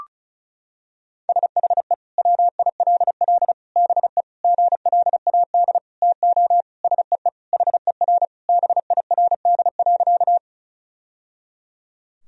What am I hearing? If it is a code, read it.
Morse code, heard as a keyed tone, about 36 wpm, 700 Hz. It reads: SHE WILL BE GLAD TO SEE HER BIRD.
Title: Sentences from Top 1000 Words - Review Part 2 - 35wpm